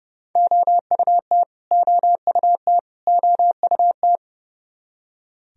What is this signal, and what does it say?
Morse code, heard as a keyed tone, about 31 wpm, 700 Hz. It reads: OUT OUT OUT
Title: Top 100 Words Repeated 30wpm